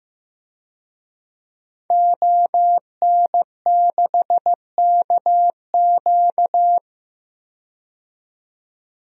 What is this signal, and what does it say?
Morse code, heard as a keyed tone, about 15 wpm, 700 Hz. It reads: ON6KQ
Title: DXCC-100 Call Signs - 5 Characters - 15wpm